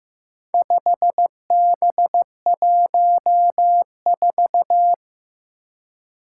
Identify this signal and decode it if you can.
Morse code, heard as a keyed tone, about 15 wpm, 700 Hz. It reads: 5B14
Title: Four Challenging Letter-Number Combinations 15wpm